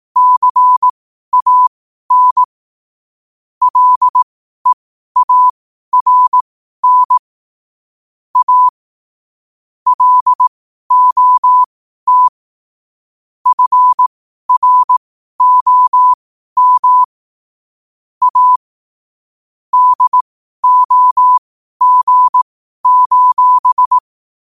Morse code, heard as a keyed tone, about 18 wpm, spaced as for 12 wpm, 1000 Hz. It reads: CAN LEARN A LOT FROM A DOG:
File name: qotdfiles / 12wpm